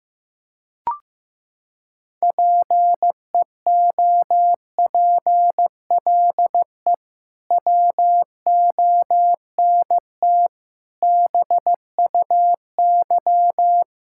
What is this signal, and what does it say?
Morse code, heard as a keyed tone, about 16 wpm, 700 Hz. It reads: PEOPLE WONT BUY
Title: Sentences from Top 1000 Words - Review Part 6 - 15wpm